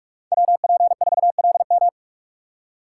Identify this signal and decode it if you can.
Morse code, heard as a keyed tone, about 47 wpm, 700 Hz. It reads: WP4LM
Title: USA Call Signs - 5 Characters - 3x Character Spacing - 45wpm